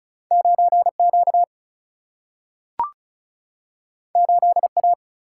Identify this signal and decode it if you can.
Morse code, heard as a keyed tone, about 36 wpm, 700 Hz. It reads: 9Q 8A
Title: Single-Letter Single-Number 35wpm